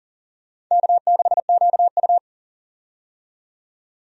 Morse code, heard as a keyed tone, about 40 wpm, 700 Hz. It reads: K6QU
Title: USA Call Signs - 4 Characters - 40wpm